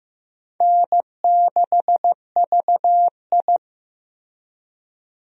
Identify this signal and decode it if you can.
Morse code, heard as a keyed tone, about 15 wpm, 700 Hz. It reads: N6VI
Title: USA Call Signs - 4 Characters - Repeated 2x - 15wpm